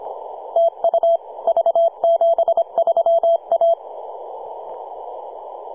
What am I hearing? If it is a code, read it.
Morse code, heard as a keyed tone, about 25 wpm, 670 Hz. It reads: TU V73A
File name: posts / 2015-sept-26-2040z-v73a-21-025-cw